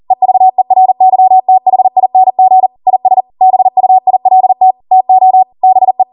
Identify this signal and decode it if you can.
Morse code, heard as a keyed tone, about 40 wpm, 750 Hz. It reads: EVERYTHING IS BUILT TO BE